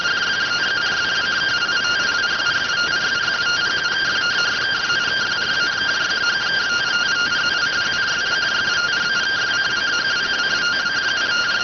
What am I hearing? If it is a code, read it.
Two-tone teleprinter signal, about 170 Hz shift.